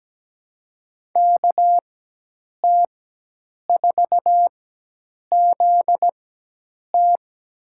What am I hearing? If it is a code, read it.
Morse code, heard as a keyed tone, about 17 wpm, spaced as for 8 wpm, 700 Hz. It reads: KT4ZT
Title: USA Call Signs - 5 Characters - 4x Character Spacing - 17wpm